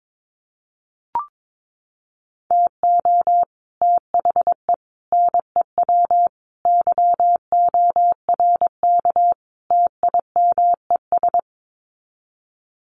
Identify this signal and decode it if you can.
Morse code, heard as a keyed tone, about 22 wpm, 700 Hz. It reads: TO THE NEW YORK TIMES
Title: Sets of 5 Words Encore2 22wpm